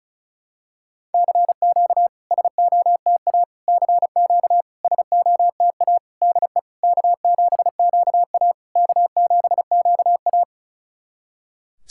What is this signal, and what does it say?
Morse code, heard as a keyed tone, about 36 wpm, 700 Hz. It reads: CQ SOTA CQ SOTA DE K7QA K7QA
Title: SOTA QSO Practice 35wpm